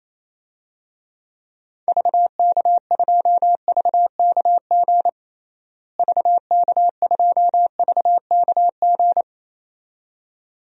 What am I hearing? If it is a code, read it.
Morse code, heard as a keyed tone, about 28 wpm, 700 Hz. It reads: VK2VKG VK2VKG
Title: DXCC-100 Call Signs - 6 Characters - Repeated 2x - 28wpm